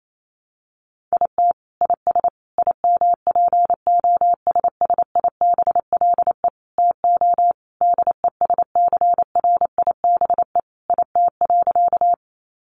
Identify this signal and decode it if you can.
Morse code, heard as a keyed tone, about 28 wpm, 700 Hz. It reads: IT IS IMPOSSIBLE TO DESCRIBE IT.
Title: Sentences from Top 1100 Words 28wpm